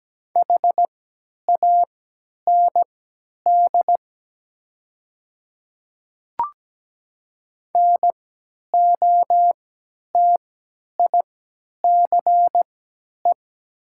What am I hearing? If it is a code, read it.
Morse code, heard as a keyed tone, about 17 wpm, spaced as for 10 wpm, 700 Hz. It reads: HAND NOTICE
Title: 3x Character Spacing - Top 500 Words - 17wpm